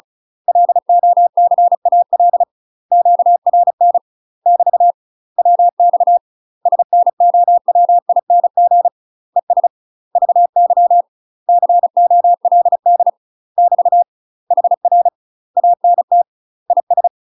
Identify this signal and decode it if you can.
Morse code, heard as a keyed tone, about 36 wpm, 700 Hz. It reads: LOCAL QRN <BT> WX SNOWING ES VY COLD <BT> HR ANT IS
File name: UK MASTER QSOs TRAINING / 35WPM / UK MASTER QSOs 0038